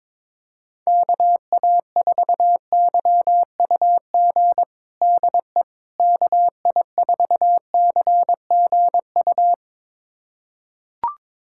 Morse code, heard as a keyed tone, about 22 wpm, 700 Hz. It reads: KA4YUG DE KI4CGU
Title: USA Callsign DE Callsign - 6 Characters - 22wpm